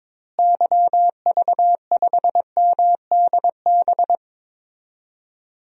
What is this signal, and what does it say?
Morse code, heard as a keyed tone, about 22 wpm, 700 Hz. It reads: YV5MDB
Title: DXCC-100 Call Signs - 6 Characters - Repeated 2x - 22wpm